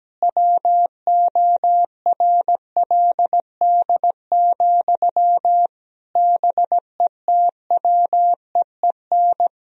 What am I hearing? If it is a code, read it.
Morse code, heard as a keyed tone, about 17 wpm, 700 Hz. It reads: WORLD, BETWEEN
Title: Sentences from Top 1000 Words - Review Part 1 - 17wpm